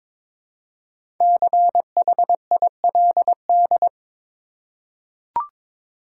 Morse code, heard as a keyed tone, about 22 wpm, 700 Hz. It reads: CHILD
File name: 5 Letter Words - First 3 Letters Given - 22wpm